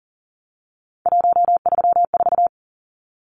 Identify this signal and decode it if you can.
Morse code, heard as a keyed tone, about 40 wpm, 700 Hz. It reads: 134
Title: Numbers - 3-Digit 40wpm